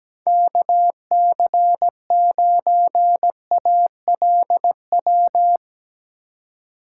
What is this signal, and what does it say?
Morse code, heard as a keyed tone, about 19 wpm, 700 Hz. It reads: KC9ALW
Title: Speed Racing - USA Call Signs 17wpm